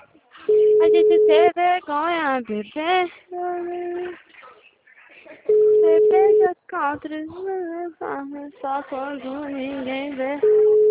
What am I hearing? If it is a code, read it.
Ringback tone (Europe, 425 Hz), with speech.